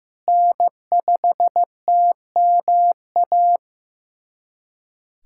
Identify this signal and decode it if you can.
Morse code, heard as a keyed tone, about 16 wpm, 700 Hz. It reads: N5TMA
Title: USA Callsign DE Callsign - 5 Characters - 15wpm